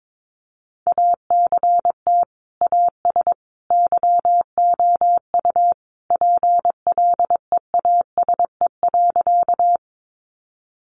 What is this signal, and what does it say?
Morse code, heard as a keyed tone, about 22 wpm, 700 Hz. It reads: ACT AS YOU PLEASE.
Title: Sentences from Top 1500 Words 22wpm